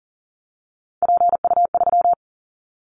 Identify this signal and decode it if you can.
Morse code, heard as a keyed tone, about 42 wpm, 700 Hz. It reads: PU3
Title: Callsign Prefixes for Top 100 DXCC - 40wpm